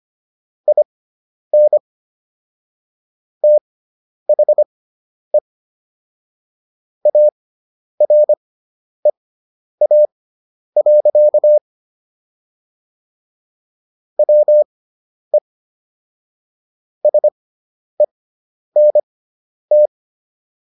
Morse code, heard as a keyed tone, about 25 wpm, spaced as for 10 wpm, 600 Hz. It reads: IN THE AREA. WE SENT